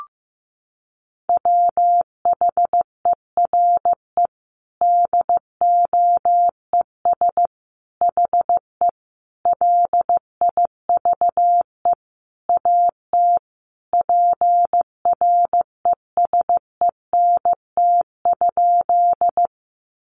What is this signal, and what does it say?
Morse code, heard as a keyed tone, about 15 wpm, 700 Hz. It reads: WHERE DOES HE LIVE AT PRESENT?
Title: Sentences from Top 800 Words 15wpm